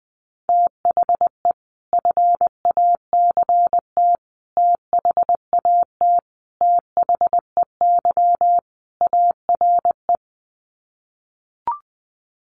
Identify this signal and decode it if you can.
Morse code, heard as a keyed tone, about 20 wpm, 700 Hz. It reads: THE FACT THAT THEY ARE
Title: Sets of 5 Words Encore2 20wpm